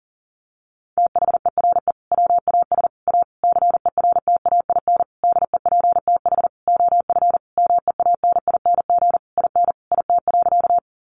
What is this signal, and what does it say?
Morse code, heard as a keyed tone, about 40 wpm, 700 Hz. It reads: THERE WAS A CERTAIN DEPTH OF MEANING IN IT.
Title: Sentences from Top 1200 Words 40wpm